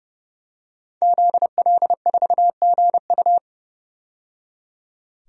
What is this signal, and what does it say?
Morse code, heard as a keyed tone, about 30 wpm, 700 Hz. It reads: ZL4GU
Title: DXCC-100 Call Signs - 5 Characters - 30wpm